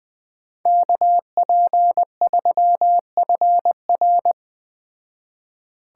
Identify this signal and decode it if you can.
Morse code, heard as a keyed tone, about 20 wpm, 700 Hz. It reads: KP3FR
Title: Speed Racing - USA 2x2 Call Signs 17wpm